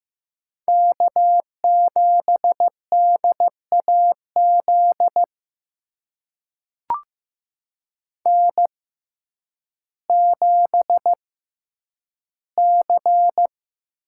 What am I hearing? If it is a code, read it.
Morse code, heard as a keyed tone, about 15 wpm, spaced as for 9 wpm, 700 Hz. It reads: K7DAZ N7C
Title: USA Call Signs - 5 Characters - 6x Character Spacing - 15wpm